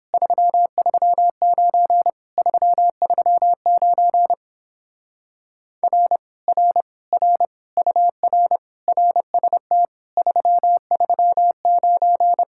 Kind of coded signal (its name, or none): Morse code